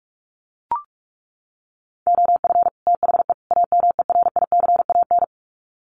Morse code, heard as a keyed tone, about 44 wpm, 700 Hz. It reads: OF THE AMERICAN
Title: Sets of 3 Words 45wpm